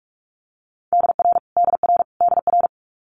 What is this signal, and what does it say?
Morse code, heard as a keyed tone, about 46 wpm, 700 Hz. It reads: DR DR DR
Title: Multi-letter Phonemes Repeated 3x 45wpm